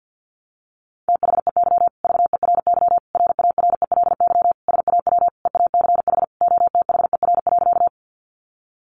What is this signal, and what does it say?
Morse code, heard as a keyed tone, about 50 wpm, 700 Hz. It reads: THEY VERY RARELY SAW EACH OTHER.
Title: Sentences from Top 1400 Words 50wpm